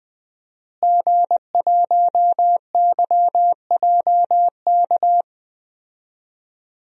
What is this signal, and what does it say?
Morse code, heard as a keyed tone, about 20 wpm, 700 Hz. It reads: G1YJK
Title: DXCC-100 Call Signs - 5 Characters - Repeated 2x - 20wpm